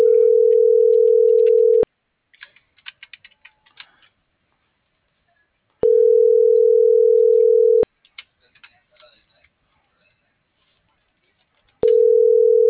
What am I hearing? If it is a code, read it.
Ringback tone (North America), 440 + 480 Hz.